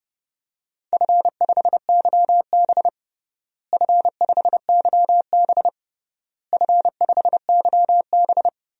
Morse code, heard as a keyed tone, about 30 wpm, 700 Hz. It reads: F5YB F5YB F5YB
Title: DXCC-100 Call Signs - 4 Characters - Repeated 3x - 30wpm